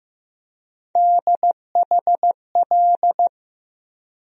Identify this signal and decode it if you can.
Morse code, heard as a keyed tone, about 15 wpm, 700 Hz. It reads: DHL
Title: USA Call Sign Postfixes - 3 Letters 15wpm